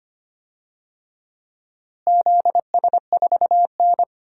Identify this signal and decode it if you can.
Morse code, heard as a keyed tone, about 25 wpm, 700 Hz. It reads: ZS4N
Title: DXCC-100 Call Signs - 4 Characters - Repeated 3x - 25wpm